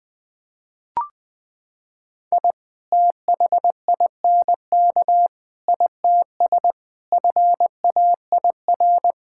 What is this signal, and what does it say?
Morse code, heard as a keyed tone, about 21 wpm, 700 Hz. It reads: I THINK ITS FAIR
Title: Sets of 6 Words - 20wpm